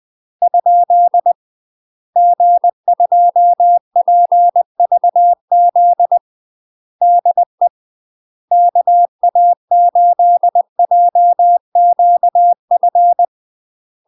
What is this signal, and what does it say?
Morse code, heard as a keyed tone, about 20 wpm, 700 Hz. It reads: ? G2PVZ DE KA8JQF
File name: UK MASTER QSOs TRAINING / 20WPM / UK MASTER QSOs 0037